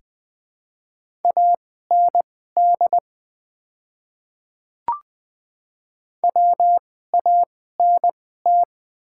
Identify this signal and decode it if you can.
Morse code, heard as a keyed tone, about 20 wpm, spaced as for 15 wpm, 700 Hz. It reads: AND WANT
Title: 2x Character Spacing - Top 200 Words - 20wpm